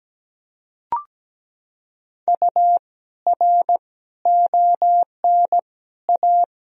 Morse code, heard as a keyed tone, about 17 wpm, 700 Hz. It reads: U R ON A
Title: Sentences from Top 400 Words and Abbreviations 17wpm